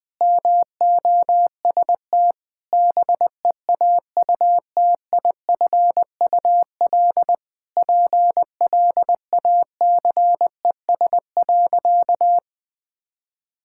Morse code, heard as a keyed tone, about 20 wpm, 700 Hz. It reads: MOST BEAUTIFUL PLACES.
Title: Sentences with 3-Word Phrases - 20wpm